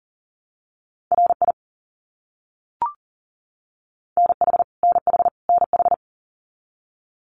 Morse code, heard as a keyed tone, about 40 wpm, 700 Hz. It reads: RI NH NH NH
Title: US States Repeated 3x 40wpm